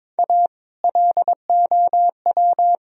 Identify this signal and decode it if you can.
Morse code, heard as a keyed tone, about 22 wpm, 700 Hz. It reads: A LOW
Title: Sentences from Top 1000 Words - Review Part 1 - 22wpm